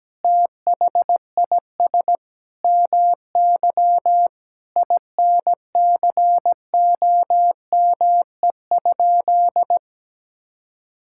Morse code, heard as a keyed tone, about 17 wpm, 700 Hz. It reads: THIS MY INCOME?
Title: Sentences from Top 1000 Words - Review Part 5 - 17wpm